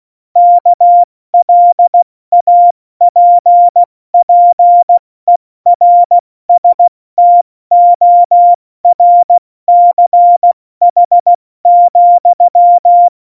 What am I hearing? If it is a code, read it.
Morse code, heard as a keyed tone, about 16 wpm, spaced as for 14 wpm, 700 Hz. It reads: KLAPPERSTORCH,
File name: Morsekurs / Witze / Witz 21 April 2020, 70 F80